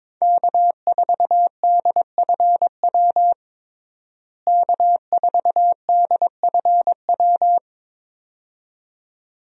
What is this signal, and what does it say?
Morse code, heard as a keyed tone, about 22 wpm, 700 Hz. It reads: K4DFW K4DFW
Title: USA Call Signs - 5 Characters - Repeated 2x - 22wpm